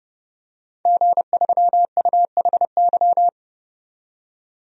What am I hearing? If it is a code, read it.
Morse code, heard as a keyed tone, about 31 wpm, 700 Hz. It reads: G3UHY
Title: DXCC-100 Call Signs - 5 Characters - 30wpm